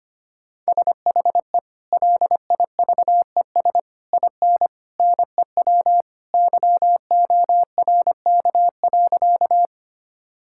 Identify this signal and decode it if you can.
Morse code, heard as a keyed tone, about 25 wpm, 700 Hz. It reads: SHE LIVES IN NEW YORK.
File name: practice / 25wpm / Sentences with 3-Word Phrases - 25wpm